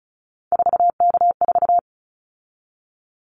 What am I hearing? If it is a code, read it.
Morse code, heard as a keyed tone, about 36 wpm, 700 Hz. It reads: 4K4
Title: Callsign Prefixes for Top 100 DXCC - 35wpm